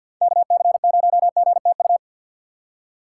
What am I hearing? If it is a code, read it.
Morse code, heard as a keyed tone, about 47 wpm, 700 Hz. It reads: KC0GTU
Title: USA Call Signs - 6 Characters - 4x Character Spacing - 50wpm